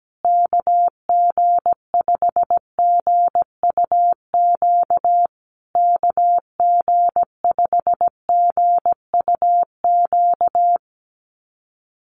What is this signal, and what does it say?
Morse code, heard as a keyed tone, about 17 wpm, 700 Hz. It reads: KG5GUQ KG5GUQ
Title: USA Call Signs Repeated 3x 17wpm